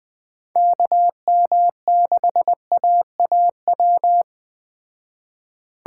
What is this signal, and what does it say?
Morse code, heard as a keyed tone, about 21 wpm, 700 Hz. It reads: KM6AAW